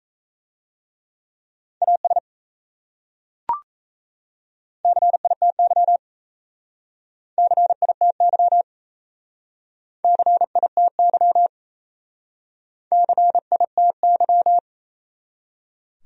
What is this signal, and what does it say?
Morse code, heard as a keyed tone, about 37 wpm, 700 Hz. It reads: AS CITY CITY CITY CITY